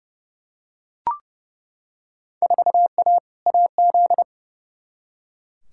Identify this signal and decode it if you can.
Morse code, heard as a keyed tone, about 30 wpm, 700 Hz. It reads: 4A AZ
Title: Field Day Exchanges - 30wpm